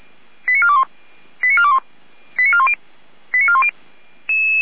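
Five-tone selective call (ZVEI), heard as quick runs of five tones, tones of about 70 ms. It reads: 89421 89421 89410 89410
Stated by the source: ZVEI five-tone selective call received with an SDR, hosted by Feuerwehr Orlamünde.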